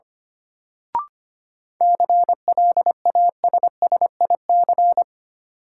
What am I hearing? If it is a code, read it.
Morse code, heard as a keyed tone, about 25 wpm, 700 Hz. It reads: CLASSIC
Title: Sentences from Top 1000 Words - New Word Drill - 25wpm